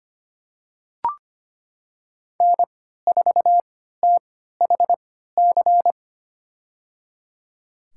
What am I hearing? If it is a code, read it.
Morse code, heard as a keyed tone, about 25 wpm, spaced as for 14 wpm, 700 Hz. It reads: N4THC
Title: USA Call Signs - 5 Characters - 3x Character Spacing - 25wpm